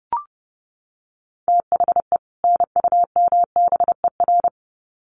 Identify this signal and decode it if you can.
Morse code, heard as a keyed tone, about 30 wpm, 700 Hz. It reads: THE NUMBER
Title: Sets of 2 Words 30wpm